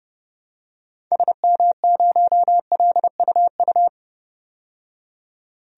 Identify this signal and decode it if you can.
Morse code, heard as a keyed tone, about 32 wpm, 700 Hz. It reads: SM0LUU
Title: DXCC-100 Call Signs - 6 Characters - 30wpm